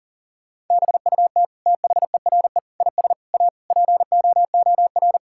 Morse code, heard as a keyed tone, about 40 wpm, 700 Hz. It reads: BUT THERE IS A POOR
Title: Sentences from Top 1000 Words - Review Part 6 - 40wpm